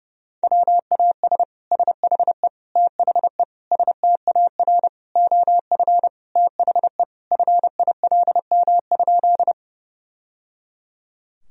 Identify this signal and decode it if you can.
Morse code, heard as a keyed tone, about 32 wpm, 700 Hz. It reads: WAS SHE THE STAR OF THE FILM?
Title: Sentences from Top 1000 Words - Review Part 3 - 30wpm